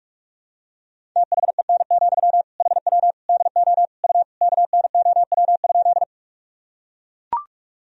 Morse code, heard as a keyed tone, about 41 wpm, 700 Hz. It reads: THEN, HW DO U KNOW?
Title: Sentences from Top 100 Words and Abbreviations 45wpm